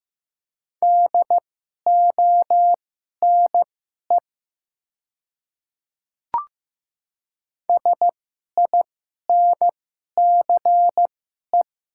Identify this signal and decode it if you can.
Morse code, heard as a keyed tone, about 15 wpm, spaced as for 11 wpm, 690 Hz. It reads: DONE SINCE
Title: 2x Character Spacing - Top 500 Words - 15wpm